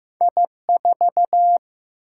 Morse code, heard as a keyed tone, about 15 wpm, 700 Hz. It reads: I4
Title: Single-Letter Single-Number 15wpm